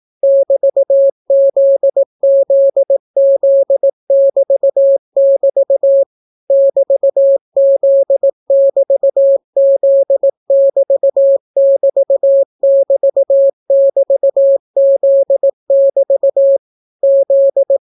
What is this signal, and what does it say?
Morse code, heard as a keyed tone, about 18 wpm, 550 Hz. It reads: <BT>ZZZ<BT><BT> <BT>Z<BT>Z<BT><BT><BT><BT>Z<BT> Z